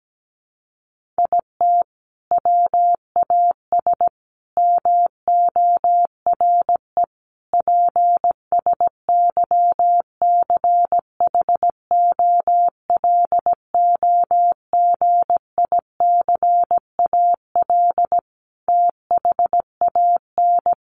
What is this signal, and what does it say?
Morse code, heard as a keyed tone, about 17 wpm, 700 Hz. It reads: IT WAS MORE PSYCHOLOGICAL THAN
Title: Sentences from Top 1400 Words 17wpm